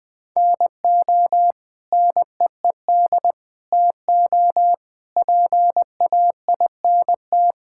Morse code, heard as a keyed tone, about 20 wpm, 700 Hz. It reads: NO NEED TO PAINT
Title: Sentences from Top 1000 Words - Review Part 6 - 20wpm